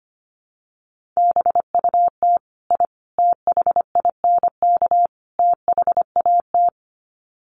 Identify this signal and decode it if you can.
Morse code, heard as a keyed tone, about 25 wpm, 700 Hz. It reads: BUT I THINK THAT
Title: Sets of 4 Words 25wpm